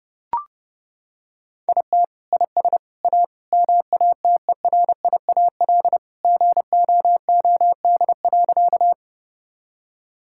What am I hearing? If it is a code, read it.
Morse code, heard as a keyed tone, about 28 wpm, 700 Hz. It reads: IT IS A MATERIAL GOOD.
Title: Sentences from Top 1000 Words - Review Part 5 - 30wpm